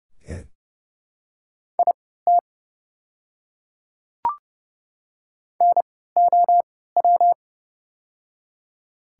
Morse code, heard as a keyed tone, about 32 wpm, spaced as for 17 wpm, 690 Hz. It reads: IT NOW
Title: 3x Character Spacing - Top 200 Words - 30wpm